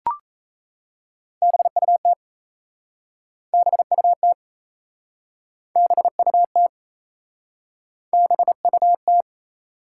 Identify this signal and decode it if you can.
Morse code, heard as a keyed tone, about 37 wpm, 700 Hz. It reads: BUT BUT BUT BUT